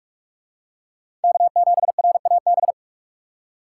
Morse code, heard as a keyed tone, about 44 wpm, 700 Hz. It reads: K7RAB